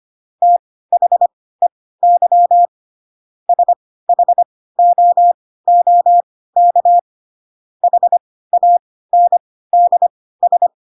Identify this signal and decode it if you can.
Morse code, heard as a keyed tone, about 25 wpm, spaced as for 16 wpm, 700 Hz. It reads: THEY SHOOK HANDS
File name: Pinocchio / Pinocchio ch2 10 25 ik1hge